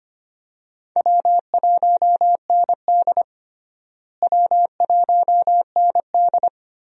Morse code, heard as a keyed tone, about 25 wpm, 700 Hz. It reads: W1ND W1ND
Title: USA Call Signs - 4 Characters - Repeated 2x - 25wpm